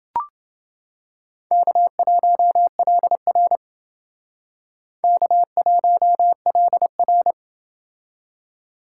Morse code, heard as a keyed tone, about 28 wpm, 700 Hz. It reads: K1LR K1LR